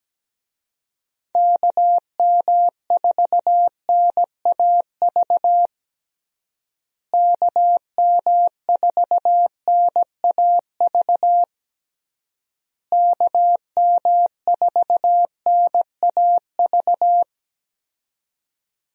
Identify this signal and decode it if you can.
Morse code, heard as a keyed tone, about 17 wpm, 700 Hz. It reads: KM4NAV KM4NAV KM4NAV
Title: USA Call Signs - 6 Characters - Repeated 3x - 17wpm